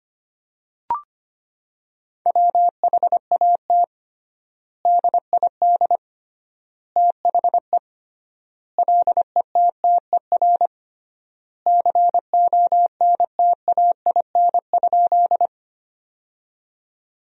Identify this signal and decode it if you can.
Morse code, heard as a keyed tone, about 25 wpm, 700 Hz. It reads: WHAT DID THE LETTER CONTAIN?
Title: 3x Word-Spacing - Sentences from Top 500 Words - 25wpm